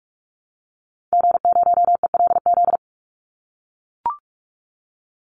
Morse code, heard as a keyed tone, about 44 wpm, 700 Hz. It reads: G0ELZ
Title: Call Signs - 45wpm